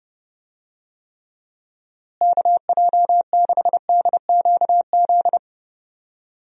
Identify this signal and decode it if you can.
Morse code, heard as a keyed tone, about 30 wpm, 700 Hz. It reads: KJ6DQZ
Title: USA Call Signs - 6 Characters - Repeated 2x - 30wpm